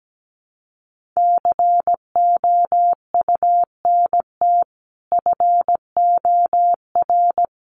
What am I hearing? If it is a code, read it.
Morse code, heard as a keyed tone, about 17 wpm, 700 Hz. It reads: COUNT FOR
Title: Sentences from Top 1500 Words 17wpm